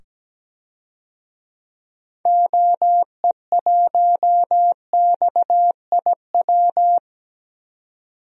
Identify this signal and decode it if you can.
Morse code, heard as a keyed tone, about 17 wpm, 700 Hz. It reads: OE1XIW
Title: DXCC-100 Call Signs - 6 Characters - 17wpm